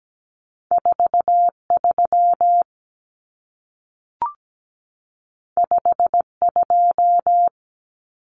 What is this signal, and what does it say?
Morse code, heard as a keyed tone, about 17 wpm, 700 Hz. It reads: 43 52
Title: Numbers - 2-Digit 17wpm